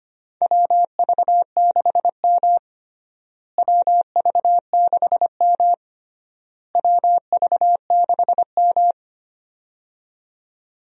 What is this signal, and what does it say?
Morse code, heard as a keyed tone, about 25 wpm, 700 Hz. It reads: WV6M WV6M WV6M
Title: USA Call Signs - 4 Characters - Repeated 3x - 25wpm